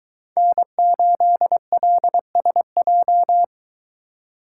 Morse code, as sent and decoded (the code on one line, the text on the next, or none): -. ---.. .-.. ... .---
N8LSJ